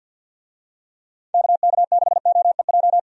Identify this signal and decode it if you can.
Morse code, heard as a keyed tone, about 51 wpm, 700 Hz. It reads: KK6OEJ
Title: USA Call Signs - 6 Characters - 50wpm